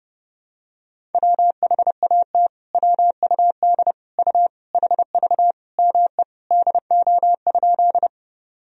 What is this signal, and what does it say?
Morse code, heard as a keyed tone, about 30 wpm, 700 Hz. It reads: WHAT WUD U HV ME DO?
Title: Sentences from Top 200 Words and Abbreviations 30wpm